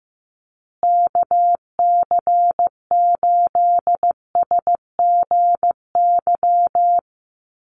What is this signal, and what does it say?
Morse code, heard as a keyed tone, about 15 wpm, 700 Hz. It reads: KC8SGY
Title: USA Call Signs Repeated 2x 15wpm